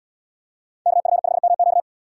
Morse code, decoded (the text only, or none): KF5MQ